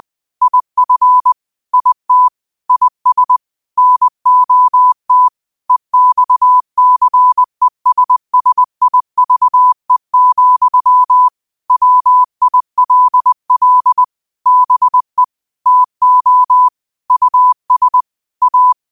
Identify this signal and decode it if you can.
Morse code, heard as a keyed tone, about 20 wpm, 1000 Hz. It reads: IF IT IS NOT EXCESSIVE, WILL BE TO US A
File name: qotdfiles / 20wpm